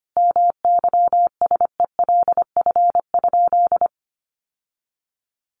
Morse code, heard as a keyed tone, about 25 wpm, 700 Hz. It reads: MYSELF?
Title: Sentences from Top 1300 Words 25wpm